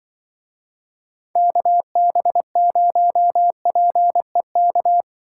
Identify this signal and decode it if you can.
Morse code, heard as a keyed tone, about 24 wpm, 700 Hz. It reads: KB0PEK
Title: Speed Racing - USA Call Signs 20wpm